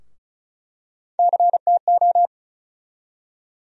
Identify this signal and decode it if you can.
Morse code, heard as a keyed tone, about 34 wpm, 700 Hz. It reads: CTO
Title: USA Call Sign Postfixes - 3 Letters 35wpm